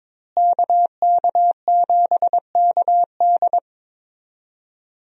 Morse code, heard as a keyed tone, about 21 wpm, 700 Hz. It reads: KK7KD
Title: USA Call Signs - 5 Characters - Repeated 3x - 22wpm